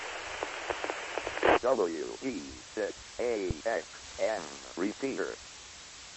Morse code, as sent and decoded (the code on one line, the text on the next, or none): none